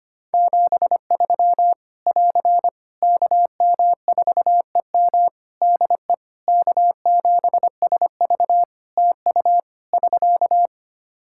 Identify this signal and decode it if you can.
Morse code, heard as a keyed tone, about 25 wpm, 700 Hz. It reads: 73 <AR> KM4EM DE K7SV TU <SK>
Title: Practice QSO - Exchange 4 - Ending - 25wpm